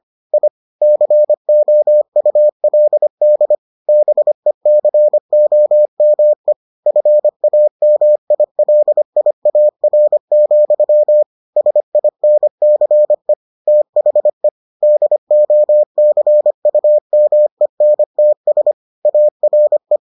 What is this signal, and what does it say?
Morse code, heard as a keyed tone, about 25 wpm, 600 Hz. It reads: I COULD BECOME FAMILIAR, SINCE THE DOCUMENTS ARE